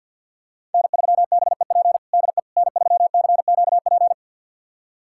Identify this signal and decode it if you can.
Morse code, heard as a keyed tone, about 51 wpm, 700 Hz. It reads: N3BEP DE N3CQP
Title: USA Callsign DE Callsign - 5 Characters - 50wpm